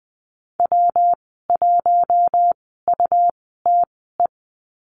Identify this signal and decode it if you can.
Morse code, heard as a keyed tone, about 20 wpm, spaced as for 15 wpm, 700 Hz. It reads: W1UTE
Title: 140 Lesson 23g - Letter f - Call Signs 20wpm